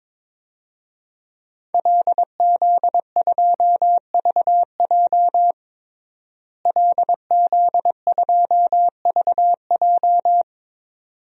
Morse code, heard as a keyed tone, about 21 wpm, 700 Hz. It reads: LZ2VJ LZ2VJ
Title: DXCC-100 Call Signs - 5 Characters - Repeated 2x - 22wpm